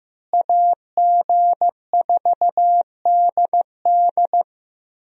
Morse code, heard as a keyed tone, about 14 wpm, 700 Hz. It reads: AG4DD
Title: USA Call Signs - 5 Characters - Repeated 3x - 15wpm